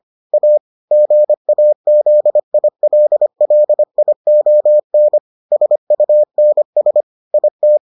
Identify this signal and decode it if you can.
Morse code, heard as a keyed tone, about 25 wpm, 600 Hz. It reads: A GAZILLION SUNS IT